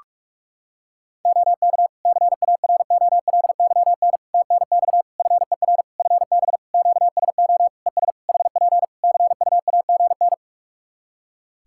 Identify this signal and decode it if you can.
Morse code, heard as a keyed tone, about 48 wpm, 700 Hz. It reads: OK CAROLYN TNX FER FB QSO ES HP CUAGN
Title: Practice QSO - Exchange 4 - Ending - 45wpm